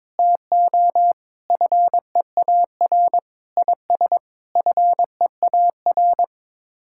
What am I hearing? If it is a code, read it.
Morse code, heard as a keyed tone, about 22 wpm, 700 Hz. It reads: TO FEAR IS FEAR
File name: Sets of 9 Words - 22wpm